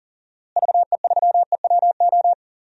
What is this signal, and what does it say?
Morse code, heard as a keyed tone, about 40 wpm, 700 Hz. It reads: VE3EWO
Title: DXCC-100 Call Signs - 6 Characters - Repeated 3x - 40wpm